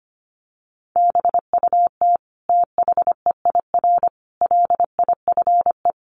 Morse code, heard as a keyed tone, about 25 wpm, 700 Hz. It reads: BUT THEIR LIFE
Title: Sentences from Top 1400 Words 25wpm